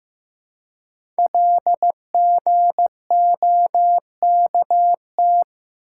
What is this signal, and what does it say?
Morse code, heard as a keyed tone, about 15 wpm, 700 Hz. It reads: LGOKT
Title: Code Groups - 5 Letters - 15wpm